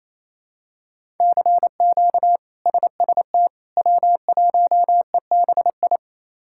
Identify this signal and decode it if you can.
Morse code, heard as a keyed tone, about 28 wpm, 700 Hz. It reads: CQ SST W1EBI
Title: K1USN SST QSO Practice - 28wpm